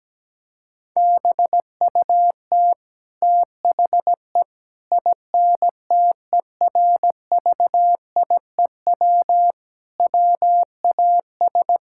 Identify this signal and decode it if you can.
Morse code, heard as a keyed tone, about 17 wpm, 700 Hz. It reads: BUT THE INTERVIEW WAS
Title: Sentences from Top 1600 Words 17wpm